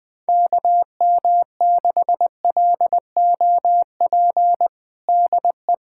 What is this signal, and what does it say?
Morse code, heard as a keyed tone, about 21 wpm, 700 Hz. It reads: KM6LOP DE
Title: USA Callsign DE Callsign - 6 Characters - 20wpm